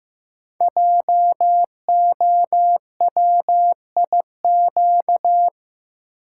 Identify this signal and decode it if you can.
Morse code, heard as a keyed tone, about 15 wpm, 700 Hz. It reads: JOWIQ
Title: Code Groups - 5 Letters - 15wpm